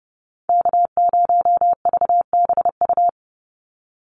Morse code, heard as a keyed tone, about 30 wpm, 700 Hz. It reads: K0VBU
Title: Call Signs - 30wpm